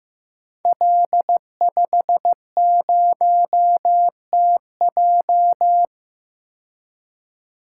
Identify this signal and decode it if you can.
Morse code, heard as a keyed tone, about 16 wpm, 700 Hz. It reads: L50TJ
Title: Code Groups - 5 Letter:Numbers - 15wpm